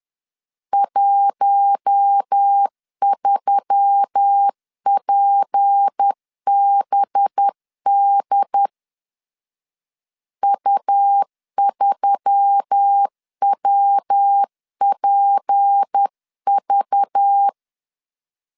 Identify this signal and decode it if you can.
Morse code, heard as a keyed tone, about 10 wpm, 800 Hz. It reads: 13PBD U3WPV
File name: CW / DET / 1 6 1500C S C 10wpm 4252